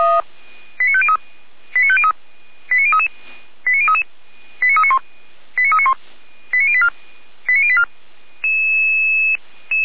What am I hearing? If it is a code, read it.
Five-tone selective call (ZVEI), heard as quick runs of five tones, tones of about 70 ms. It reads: 89503 89503 89033 89033 89391 89391 89084 89084